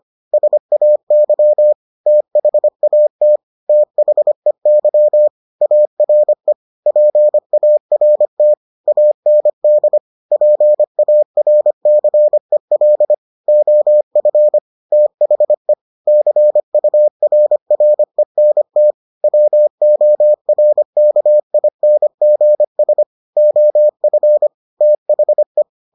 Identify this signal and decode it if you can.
Morse code, heard as a keyed tone, about 25 wpm, 600 Hz. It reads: SAY THAT THEY ARE PART AND PARCEL OF THE CURRENT WORKINGS OF THE